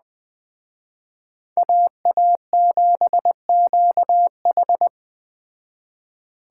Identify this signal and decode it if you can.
Morse code, heard as a keyed tone, about 20 wpm, 700 Hz. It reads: AA7QH
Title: Speed Racing - USA 2x2 Call Signs 17wpm